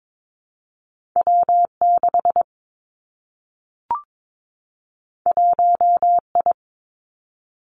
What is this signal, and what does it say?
Morse code, heard as a keyed tone, about 22 wpm, 700 Hz. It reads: W 6 1 I
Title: Two Challenging Letter-Number Combinations 22wpm